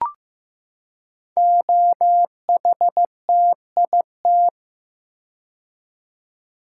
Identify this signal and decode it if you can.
Morse code, heard as a keyed tone, about 15 wpm, 700 Hz. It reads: OHTIT
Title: Code Groups - 5 Letters - 15wpm